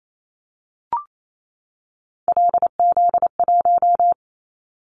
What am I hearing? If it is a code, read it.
Morse code, heard as a keyed tone, about 28 wpm, 700 Hz. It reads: LZ1
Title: Callsign Prefixes for Top 25 DXCC - 28wpm